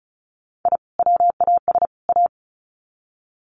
Morse code, heard as a keyed tone, about 34 wpm, 700 Hz. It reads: I WAS A
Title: Sets of 3 Words 35wpm